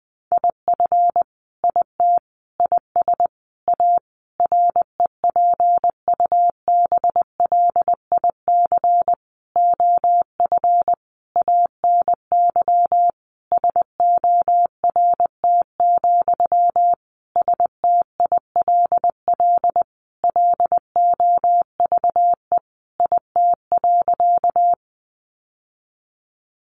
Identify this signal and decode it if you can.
Morse code, heard as a keyed tone, about 19 wpm, 700 Hz. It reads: IF IT IS A REPUBLIC OF ANY SORT, STILL LOVE IT.
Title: Sentences from Top 1500 Words 20wpm